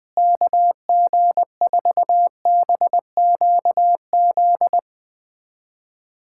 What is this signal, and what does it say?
Morse code, heard as a keyed tone, about 21 wpm, 700 Hz. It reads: KG4BQZ
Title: USA Callsign DE Callsign 20wpm